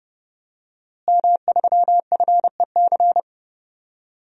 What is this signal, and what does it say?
Morse code, heard as a keyed tone, about 30 wpm, 700 Hz. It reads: M3FEC
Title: DXCC-100 Call Signs - 5 Characters - Repeated 2x - 30wpm